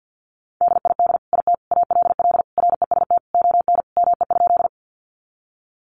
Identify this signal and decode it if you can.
Morse code, heard as a keyed tone, about 50 wpm, 700 Hz. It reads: DID IT ALL REST ON ME?
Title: Sentences from Top 500 Words 50wpm